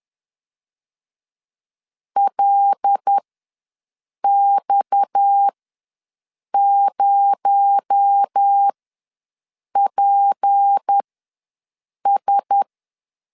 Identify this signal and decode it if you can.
Morse code, heard as a keyed tone, about 10 wpm, spaced as for 6 wpm, 800 Hz. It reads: LX0PS
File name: LEZIONI / 7 GF9Z / 7 G ws 10wpm 3843